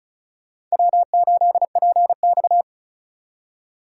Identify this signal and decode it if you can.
Morse code, heard as a keyed tone, about 36 wpm, 700 Hz. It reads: W8PX
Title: USA Call Signs - 4 Characters - 3x Character Spacing - 35wpm